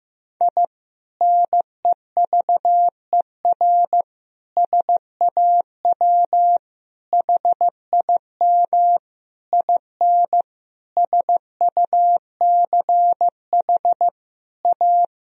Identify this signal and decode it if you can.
Morse code, heard as a keyed tone, about 15 wpm, 700 Hz. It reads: I NEVER SAW HIM IN SUCH A
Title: Sentences from Top 1000 Words - Review Part 2 - 15wpm